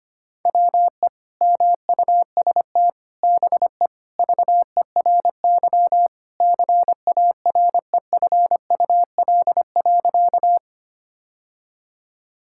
Morse code, heard as a keyed tone, about 25 wpm, 700 Hz. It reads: WE MUST BE VERY CAREFUL.
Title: Sentences from Top 1600 Words 25wpm